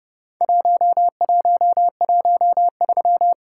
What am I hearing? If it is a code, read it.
Morse code, heard as a keyed tone, about 30 wpm, 700 Hz. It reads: 1113
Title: Worked All Europe DX Contest - QTC Exchange - 30wpm